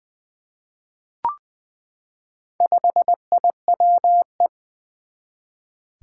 Morse code, heard as a keyed tone, about 20 wpm, 700 Hz. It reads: 5IWE
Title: Four Challenging Letter-Number Combinations 20wpm